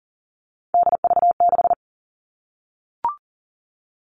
Morse code, heard as a keyed tone, about 40 wpm, 700 Hz. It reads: DV6
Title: All Cut Numbers - 3-Digit 40wpm